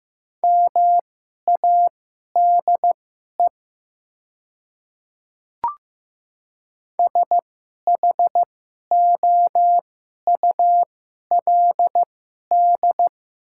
Morse code, heard as a keyed tone, about 16 wpm, spaced as for 11 wpm, 700 Hz. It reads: MADE SHOULD
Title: 2x Character Spacing - Top 300 Words - 15wpm